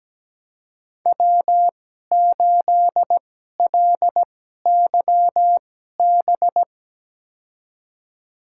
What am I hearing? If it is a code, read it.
Morse code, heard as a keyed tone, about 17 wpm, spaced as for 12 wpm, 700 Hz. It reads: W8LYB
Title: USA Call Signs - 5 Characters - 2x Character Spacing - 17wpm